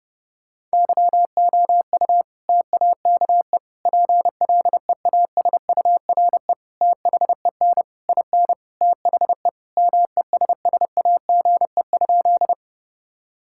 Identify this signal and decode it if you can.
Morse code, heard as a keyed tone, about 30 wpm, 700 Hz. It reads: YOU TAKE PLEASURE THEN IN THE MESSAGE?
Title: Sentences from Top 1600 Words 30wpm